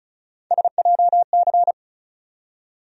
Morse code, heard as a keyed tone, about 36 wpm, 700 Hz. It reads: SJC
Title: USA Call Sign Postfixes - 3 Letters 35wpm